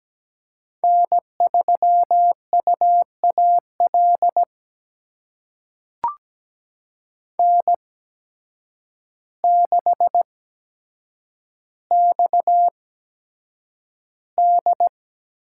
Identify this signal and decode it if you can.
Morse code, heard as a keyed tone, about 17 wpm, spaced as for 8 wpm, 700 Hz. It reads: N3UAL N6XD
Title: USA Call Signs - 5 Characters - 8x Character Spacing - 17wpm